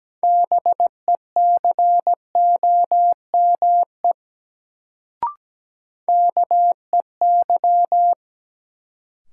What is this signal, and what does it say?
Morse code, heard as a keyed tone, about 17 wpm, 700 Hz. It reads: BECOME KEY
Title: Sentences from Top 600 Words - New Word Drill - 17wpm